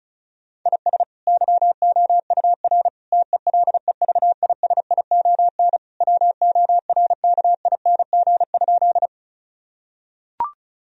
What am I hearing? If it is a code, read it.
Morse code, heard as a keyed tone, about 36 wpm, 700 Hz. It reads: IS YOUR TELEVISION WORKING?
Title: Sentences from Top 1000 Words - Review Part 4 - 35wpm